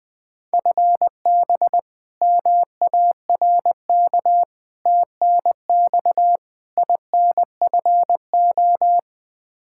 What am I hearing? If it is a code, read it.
Morse code, heard as a keyed tone, about 20 wpm, 700 Hz. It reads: FB MARK TNX INFO
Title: K1USN SST QSO Practice 20wpm